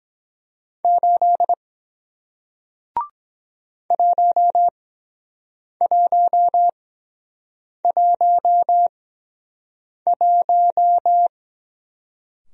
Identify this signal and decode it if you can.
Morse code, heard as a keyed tone, about 23 wpm, 700 Hz. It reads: 8 1 1 1 1